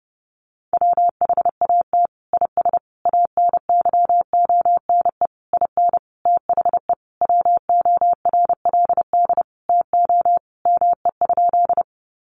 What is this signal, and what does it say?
Morse code, heard as a keyed tone, about 30 wpm, 700 Hz. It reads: WHAT IS ANYONE IN THE WORLD TO ME?
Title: Sentences from Top 900 Words 30wpm